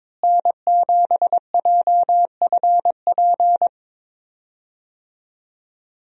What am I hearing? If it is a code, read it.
Morse code, heard as a keyed tone, about 21 wpm, 700 Hz. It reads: N7JFP
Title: USA Call Signs - 5 Characters - Repeated 3x - 22wpm